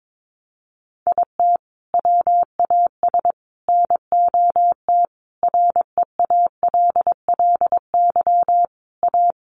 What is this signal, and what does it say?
Morse code, heard as a keyed tone, about 22 wpm, 700 Hz. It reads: IT WAS NOT REALLY A
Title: Sentences from Top 600 Words 22wpm